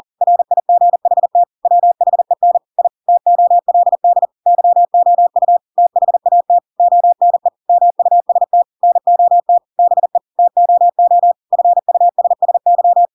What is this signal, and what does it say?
Morse code, heard as a keyed tone, about 40 wpm, 700 Hz. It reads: RIGHT WHEN I TOLD YOU THAT ONE MUST NOT BE TOO FUSSY
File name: Pinocchio / Pinocchio ch7 40 40 ik1hge